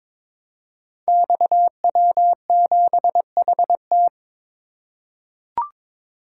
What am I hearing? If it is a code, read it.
Morse code, heard as a keyed tone, about 22 wpm, 700 Hz. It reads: XW7HT
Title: Code Groups - 5 Letter:Numbers - 22wpm